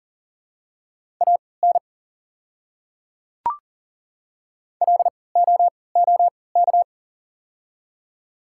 Morse code, heard as a keyed tone, about 40 wpm, spaced as for 23 wpm, 700 Hz. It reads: AN LOOK
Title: 3x Character Spacing - Top 100 Words - 40wpm